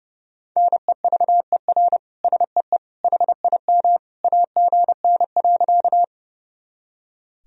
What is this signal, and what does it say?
Morse code, heard as a keyed tone, about 29 wpm, 700 Hz. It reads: NEVER SEE HIM AGN.
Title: Sentences from Top 300 Words and Abbreviations 30wpm